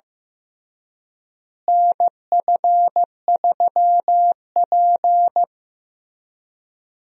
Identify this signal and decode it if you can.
Morse code, heard as a keyed tone, about 15 wpm, 700 Hz. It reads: NF3P
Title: USA Call Signs - 4 Characters - Repeated 2x - 15wpm